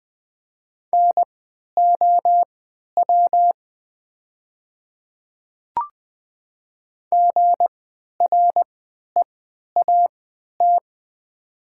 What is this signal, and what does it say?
Morse code, heard as a keyed tone, about 20 wpm, spaced as for 11 wpm, 700 Hz. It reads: NOW GREAT
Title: 3x Character Spacing - Top 500 Words - 20wpm